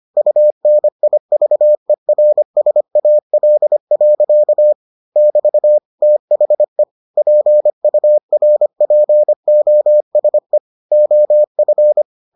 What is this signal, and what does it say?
Morse code, heard as a keyed tone, about 25 wpm, 600 Hz. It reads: UNIVERSAL. <BT> THE PURPOSE OF